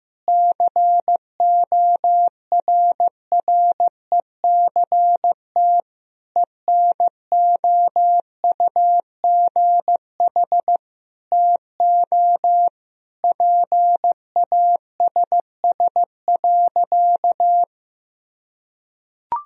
Morse code, heard as a keyed tone, about 15 wpm, 700 Hz. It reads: CORRECT ENOUGH TO PASS.
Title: Sentences from Top 1000 Words - Review Part 2 - 15wpm